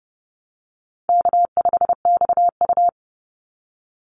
Morse code, heard as a keyed tone, about 30 wpm, 700 Hz. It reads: K5XU
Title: Easy Call Signs Repeated 2x - 30wpm